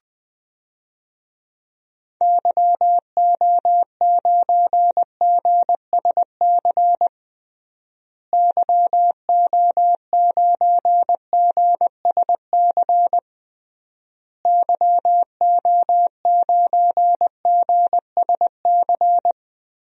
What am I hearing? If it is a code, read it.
Morse code, heard as a keyed tone, about 21 wpm, 700 Hz. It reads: YO9GSC YO9GSC YO9GSC
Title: DXCC-100 Call Signs - 6 Characters - Repeated 3x - 20wpm